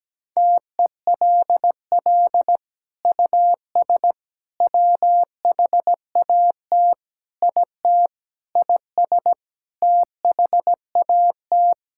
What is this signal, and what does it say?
Morse code, heard as a keyed tone, about 17 wpm, 700 Hz. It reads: TELL US WHAT IT IS THAT
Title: Sentences from Top 1000 Words - Review Part 6 - 17wpm